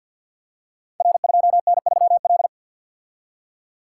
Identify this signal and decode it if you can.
Morse code, heard as a keyed tone, about 50 wpm, 700 Hz. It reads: R2N3L
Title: Code Groups - 5 Letter:Numbers - 50wpm